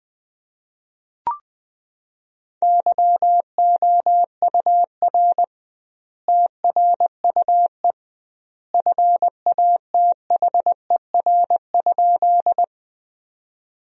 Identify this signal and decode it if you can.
Morse code, heard as a keyed tone, about 19 wpm, 700 Hz. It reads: YOUR TRUE FATHER?
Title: 2x Word-Spacing - Sentences from Top 400 Words - 20wpm